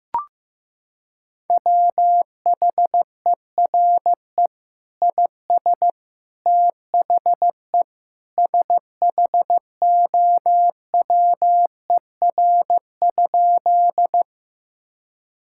Morse code, heard as a keyed tone, about 15 wpm, 700 Hz. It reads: WHERE IS THE SHOWER?
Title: Sentences from Top 1600 Words 15wpm